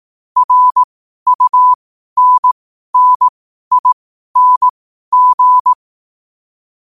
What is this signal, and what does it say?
Morse code, heard as a keyed tone, about 18 wpm, spaced as for 12 wpm, 1000 Hz. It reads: RUNNING
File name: qotdfiles / 12wpm